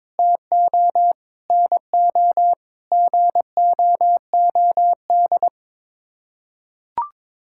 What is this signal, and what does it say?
Morse code, heard as a keyed tone, about 22 wpm, 700 Hz. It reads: TO NO GOOD
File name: Sets of 9 Words - 22wpm